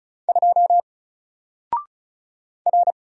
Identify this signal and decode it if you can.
Morse code, heard as a keyed tone, about 36 wpm, 700 Hz. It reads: 2 R
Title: Single Letter-Number 35wpm